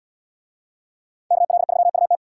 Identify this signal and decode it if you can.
Morse code, heard as a keyed tone, about 70 wpm, 700 Hz. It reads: KL3UT